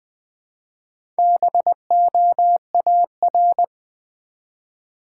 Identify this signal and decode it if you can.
Morse code, heard as a keyed tone, about 20 wpm, 700 Hz. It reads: BOAR
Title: Build A Word - 1 Letter At A Time - 3 to 5-Letter Words - 20wpm